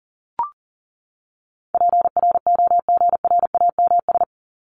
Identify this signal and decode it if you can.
Morse code, heard as a keyed tone, about 38 wpm, 700 Hz. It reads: PROGRAMS
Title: ES:S Suffix - Top 100 Words with Base Words 40wpm